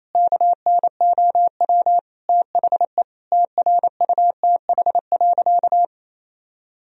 Morse code, heard as a keyed tone, about 27 wpm, 700 Hz. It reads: KNOW THE TRUTH.
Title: Sentences from Top 1000 Words - Review Part 3 - 28wpm